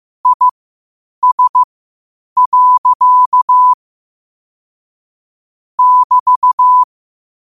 Morse code, heard as a keyed tone, about 15 wpm, spaced as for 8 wpm, 1000 Hz. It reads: IS. <BT>
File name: qotdfiles / 8wpm